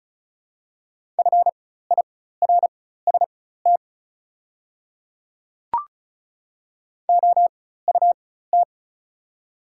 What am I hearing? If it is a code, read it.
Morse code, heard as a keyed tone, about 36 wpm, spaced as for 16 wpm, 690 Hz. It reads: FIRST OUT
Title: 4x Character Spacing - Top 100 Words - 35wpm